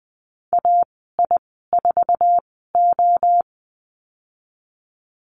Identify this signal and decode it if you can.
Morse code, heard as a keyed tone, about 20 wpm, 700 Hz. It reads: A I 4 O
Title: 062 Lesson 11g - Letter h - Call Signs 20wpm